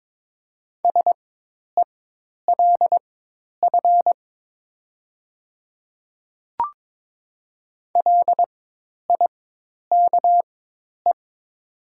Morse code, heard as a keyed tone, about 22 wpm, spaced as for 10 wpm, 700 Hz. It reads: SELF LIKE
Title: 4x Character Spacing - Top 300 Words - 22wpm